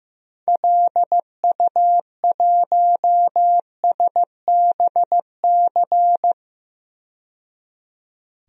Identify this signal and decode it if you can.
Morse code, heard as a keyed tone, about 15 wpm, 700 Hz. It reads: LU1SBC
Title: DXCC-100 Call Signs - 6 Characters - 15wpm